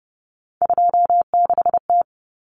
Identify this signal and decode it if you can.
Morse code, heard as a keyed tone, about 30 wpm, 700 Hz. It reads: 26T
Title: Common Cut Numbers - 3-Digit 30wpm